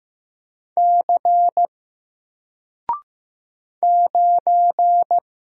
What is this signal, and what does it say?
Morse code, heard as a keyed tone, about 15 wpm, 700 Hz. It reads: C 9
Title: Single Letter-Number Alt 15wpm